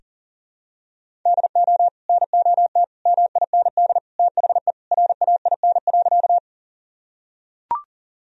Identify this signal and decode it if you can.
Morse code, heard as a keyed tone, about 37 wpm, 700 Hz. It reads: DO NOT MIND THE RAIN.
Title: Sentences from Top 1000 Words - Review Part 1 - 40wpm